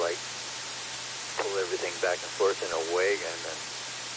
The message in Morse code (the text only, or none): none